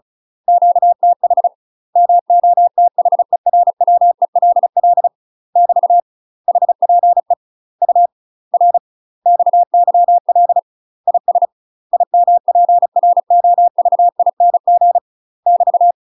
Morse code, heard as a keyed tone, about 35 wpm, 700 Hz. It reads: QTH MOTHERWELL <BT> HPE UR XYL IS IMPROVING <BT>
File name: UK MASTER QSOs TRAINING/35WPM